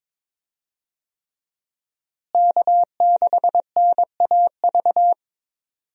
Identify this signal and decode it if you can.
Morse code, heard as a keyed tone, about 22 wpm, 700 Hz. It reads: K6NAV
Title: USA Call Signs - 5 Characters - Repeated 3x - 22wpm